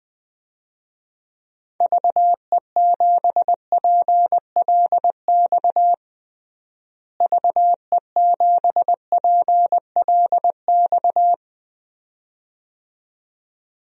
Morse code, heard as a keyed tone, about 20 wpm, 700 Hz. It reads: VE7PLX VE7PLX
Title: DXCC-100 Call Signs - 6 Characters - Repeated 2x - 20wpm